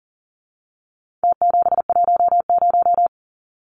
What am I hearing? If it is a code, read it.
Morse code, heard as a keyed tone, about 40 wpm, 700 Hz. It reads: T710
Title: Numbers - Contest Style - 4-Digit 40wpm